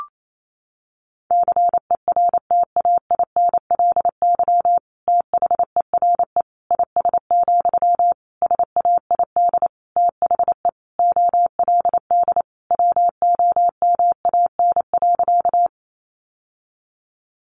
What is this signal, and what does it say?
Morse code, heard as a keyed tone, about 28 wpm, 700 Hz. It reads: CERTAINLY THERE IS, SAID THE OLD WOMAN.
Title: Sentences from Top 1100 Words 28wpm